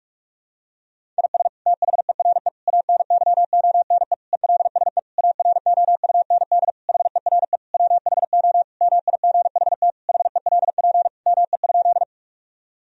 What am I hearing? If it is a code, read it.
Morse code, heard as a keyed tone, about 41 wpm, 700 Hz. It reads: IS THERE ANYONE ELSE AROUND HERE WHO MIGHT HELP ME?
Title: Sentences from Top 1000 Words - Review Part 3 - 45wpm